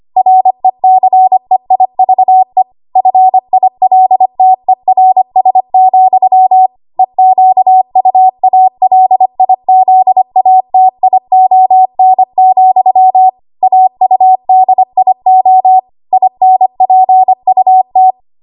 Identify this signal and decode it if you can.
Morse code, heard as a keyed tone, about 25 wpm, 750 Hz. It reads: RECEIVE FILTERS, EQUALIZATION, AUDIO INPUT
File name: Archive / 25 WPM / 25WPM